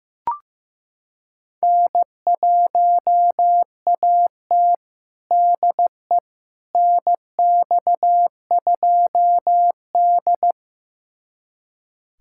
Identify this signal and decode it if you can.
Morse code, heard as a keyed tone, about 16 wpm, 700 Hz. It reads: N1AT DE NX2D
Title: USA Callsign DE Callsign - 4 Characters - 15wpm